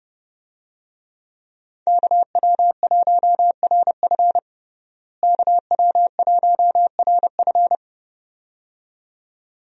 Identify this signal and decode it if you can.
Morse code, heard as a keyed tone, about 30 wpm, 700 Hz. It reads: KW1RF KW1RF
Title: USA Call Signs - 5 Characters - Repeated 2x - 30wpm